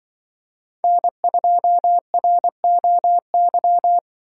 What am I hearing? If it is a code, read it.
Morse code, heard as a keyed tone, about 23 wpm, 700 Hz. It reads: N2ROY
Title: Speed Racing - USA 1x3 Call Signs 20wpm